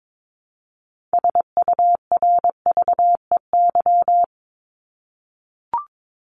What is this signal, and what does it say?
Morse code, heard as a keyed tone, about 22 wpm, 700 Hz. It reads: SURVEY
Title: Top 1501-1750 Words 22wpm